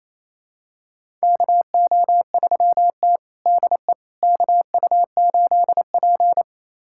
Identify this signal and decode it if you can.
Morse code, heard as a keyed tone, about 28 wpm, 700 Hz. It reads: KO3T DE KU8P
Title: USA Callsign DE Callsign - 4 Characters - 28wpm